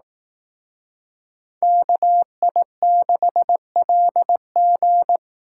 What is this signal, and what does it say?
Morse code, heard as a keyed tone, about 18 wpm, 700 Hz. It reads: KI6LG
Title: Speed Racing - USA 2x2 Call Signs 15wpm